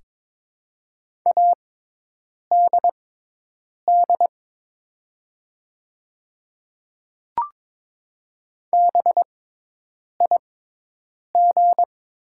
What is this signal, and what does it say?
Morse code, heard as a keyed tone, about 22 wpm, spaced as for 8 wpm, 700 Hz. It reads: ADD BIG
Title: 6x Character Spacing - Top 200 Words - 22wpm